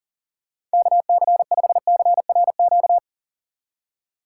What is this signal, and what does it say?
Morse code, heard as a keyed tone, about 40 wpm, 700 Hz. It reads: KC5CRQ
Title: USA Call Signs - 6 Characters - 4x Character Spacing - 40wpm